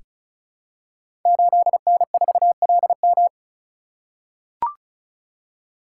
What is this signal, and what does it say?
Morse code, heard as a keyed tone, about 36 wpm, 700 Hz. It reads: 8N4LM
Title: Code Groups - 5 Letter:Numbers - 35wpm